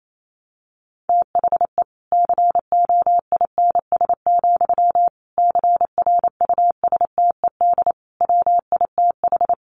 Morse code, heard as a keyed tone, about 28 wpm, 700 Hz. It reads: THE COINS, CRUSTED WITH
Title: A Sound Of Thunder 28wpm